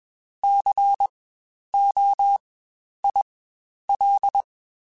Morse code, heard as a keyed tone, about 20 wpm, spaced as for 10 wpm, 800 Hz. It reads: COIL
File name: Spaced Alphas / Spaced Alphas mp3s / Cs sp @20